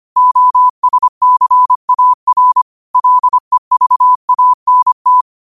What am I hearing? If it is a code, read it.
Morse code, heard as a keyed tone, about 25 wpm, 1000 Hz. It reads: OSCAR LEVANT